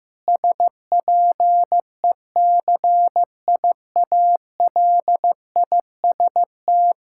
Morse code, heard as a keyed tone, about 15 wpm, 700 Hz. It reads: SPECIALIST
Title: Practice QSO - Exchange 3 - More Info - 15wpm